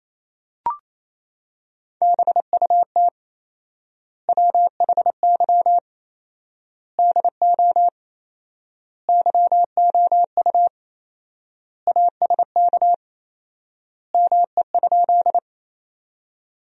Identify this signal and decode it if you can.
Morse code, heard as a keyed tone, about 28 wpm, 700 Hz. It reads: BUT WHY DO YOU ASK ME?
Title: 4x Word-Spacing - Sentences from Top 200 Words - 28wpm